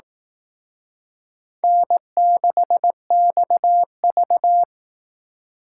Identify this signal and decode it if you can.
Morse code, heard as a keyed tone, about 18 wpm, 700 Hz. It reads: N6XV
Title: Speed Racing - USA 1x2 Call Signs 15wpm